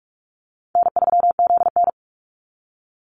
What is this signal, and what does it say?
Morse code, heard as a keyed tone, about 44 wpm, 700 Hz. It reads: N3ZN
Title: Easy Call Signs - 45wpm